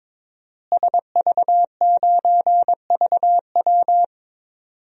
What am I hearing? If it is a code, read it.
Morse code, heard as a keyed tone, about 22 wpm, 700 Hz. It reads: SV9VW
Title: DXCC-100 Call Signs - 5 Characters - Repeated 3x - 22wpm